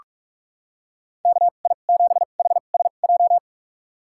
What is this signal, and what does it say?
Morse code, heard as a keyed tone, about 41 wpm, spaced as for 31 wpm, 700 Hz. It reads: KI7HSJ
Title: USA Call Signs - 6 Characters - 2x Character Spacing - 45wpm